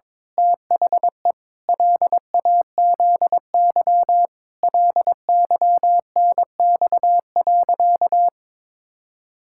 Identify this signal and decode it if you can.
Morse code, heard as a keyed tone, about 22 wpm, 700 Hz. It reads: THE LAZY LYNX.
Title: Pangram Challenge v2 - 22wpm